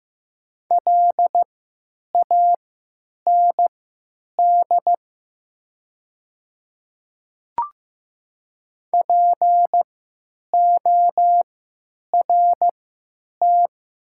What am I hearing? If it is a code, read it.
Morse code, heard as a keyed tone, about 15 wpm, spaced as for 9 wpm, 700 Hz. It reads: LAND PORT
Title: 3x Character Spacing - Top 300 Words - 15wpm